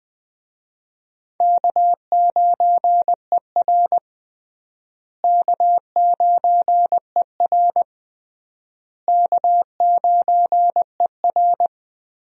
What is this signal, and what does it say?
Morse code, heard as a keyed tone, about 20 wpm, 700 Hz. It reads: K9ER K9ER K9ER
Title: USA Call Signs - 4 Characters - Repeated 3x - 20wpm